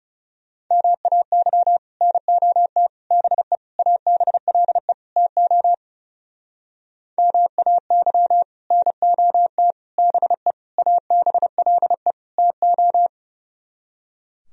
Morse code, heard as a keyed tone, about 36 wpm, 700 Hz. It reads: MAY NOT BE ABLE TO MAY NOT BE ABLE TO
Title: Speed-Racing - Sets of 5 Words 30wpm